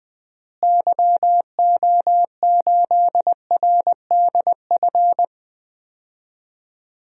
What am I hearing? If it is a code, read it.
Morse code, heard as a keyed tone, about 20 wpm, 700 Hz. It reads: YO8RDF
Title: DXCC-100 Call Signs - 6 Characters - Repeated 2x - 20wpm